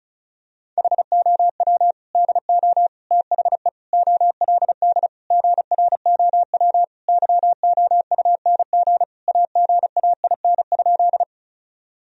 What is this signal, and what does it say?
Morse code, heard as a keyed tone, about 34 wpm, 700 Hz. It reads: HOW DO THE OLD GROW YOUNG AGAIN?
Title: Sentences from Top 1000 Words - Review Part 1 - 35wpm